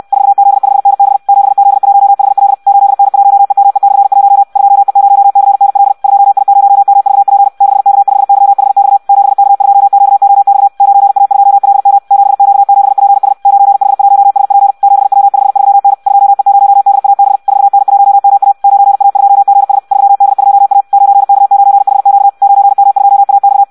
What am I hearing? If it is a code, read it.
Morse code, heard as a keyed tone, about 64 wpm, 800 Hz. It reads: 4Z5NB 7Z1HL 8N1EME/1 2E0CNL 2I0MVP 6K5YHX 7K2QOX 9A2LM 7Q7PS 9H1SP 7W53N 2E0DAB 4N1MA 9A2GS 3D2A 9K9HQ 9M2AX
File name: Audio practica CW / qrq-tx-66 2